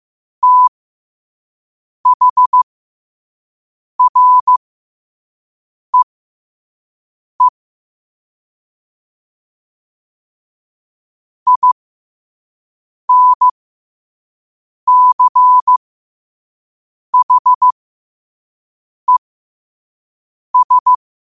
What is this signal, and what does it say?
Morse code, heard as a keyed tone, about 15 wpm, spaced as for 5 wpm, 1000 Hz. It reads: THREE INCHES